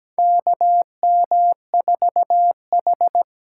Morse code, heard as a keyed tone, about 17 wpm, 700 Hz. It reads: KM4H
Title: USA Call Signs - 6 Characters - 8x Character Spacing - 17wpm